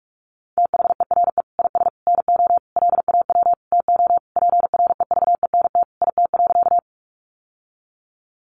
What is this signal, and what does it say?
Morse code, heard as a keyed tone, about 44 wpm, 700 Hz. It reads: THERE IS NO LAW TO PREVENT IT.
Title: Sentences from Top 1100 Words 45wpm